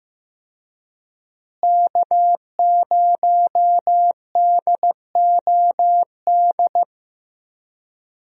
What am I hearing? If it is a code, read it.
Morse code, heard as a keyed tone, about 15 wpm, 700 Hz. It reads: K0DOD
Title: USA Call Signs - 5 Characters - Repeated 3x - 15wpm